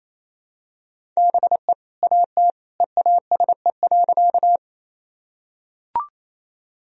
Morse code, heard as a keyed tone, about 28 wpm, 700 Hz. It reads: BE AT EASE.
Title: Sentences from Top 1000 Words - Review Part 1 - 28wpm